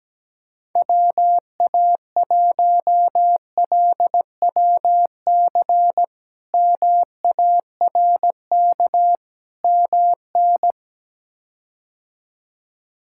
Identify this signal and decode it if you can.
Morse code, heard as a keyed tone, about 17 wpm, 700 Hz. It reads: WA1LWC MARK MN
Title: K1USN SST QSO Practice - 17wpm